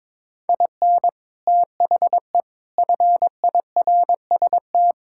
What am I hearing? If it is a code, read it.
Morse code, heard as a keyed tone, about 22 wpm, 700 Hz. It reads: IN THE FIRST
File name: practice / 22wpm / Sentences with 8-Word Phrases - 22wpm